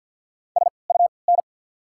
Morse code, heard as a keyed tone, about 53 wpm, spaced as for 29 wpm, 700 Hz. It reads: SUN
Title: 3x Character Spacing - Top 300 Words - 50wpm